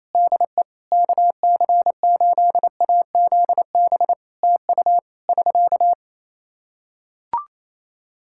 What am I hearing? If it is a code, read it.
Morse code, heard as a keyed tone, about 28 wpm, 700 Hz. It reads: DE KC8AZB TU <SK>
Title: Practice QSO - Exchange 4 - Ending - 28wpm